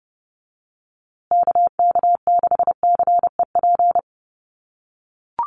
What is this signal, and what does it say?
Morse code, heard as a keyed tone, about 30 wpm, 700 Hz. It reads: KK6CEP
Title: USA Call Signs Repeated 3x 30wpm